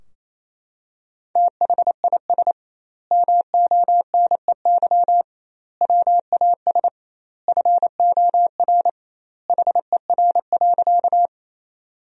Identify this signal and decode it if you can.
Morse code, heard as a keyed tone, about 28 wpm, 700 Hz. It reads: THIS MONEY WAS FOR HER.
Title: 2x Word-Spacing - Sentences from Top 400 Words - 28wpm